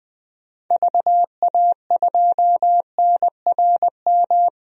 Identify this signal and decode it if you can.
Morse code, heard as a keyed tone, about 21 wpm, 700 Hz. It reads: VA2NRM
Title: DXCC-100 Call Signs - 6 Characters - 20wpm